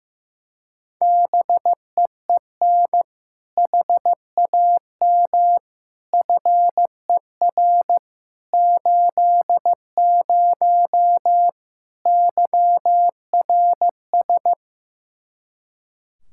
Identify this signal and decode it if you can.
Morse code, heard as a keyed tone, about 15 wpm, 700 Hz. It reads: BEEN HAM FER 80 YRS
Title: K1USN SST QSO Practice 15wpm